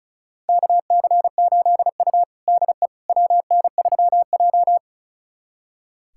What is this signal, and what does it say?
Morse code, heard as a keyed tone, about 36 wpm, 700 Hz. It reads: KC8U DE WN3J
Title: USA Callsign DE Callsign - 4 Characters - 35wpm